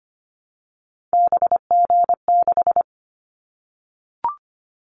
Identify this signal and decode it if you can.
Morse code, heard as a keyed tone, about 25 wpm, 700 Hz. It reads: BG6
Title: Callsign Prefixes for Top 100 DXCC - 25wpm